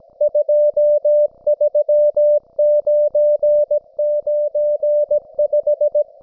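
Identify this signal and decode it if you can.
Morse code, heard as a keyed tone, about 17 wpm, 590 Hz. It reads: 23995
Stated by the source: Russian military Morse station, M32, RDL, on 5268 kHz.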